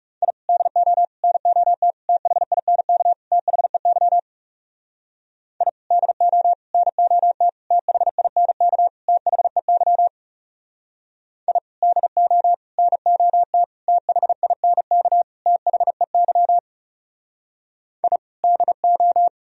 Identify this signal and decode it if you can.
Morse code, heard as a keyed tone, about 39 wpm, 700 Hz. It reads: I DO NOT THINK THEY I DO NOT THINK THEY I DO NOT THINK THEY I DO